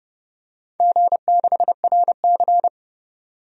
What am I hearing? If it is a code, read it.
Morse code, heard as a keyed tone, about 31 wpm, 700 Hz. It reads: G6RC
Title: DXCC-100 Call Signs - 4 Characters - 30wpm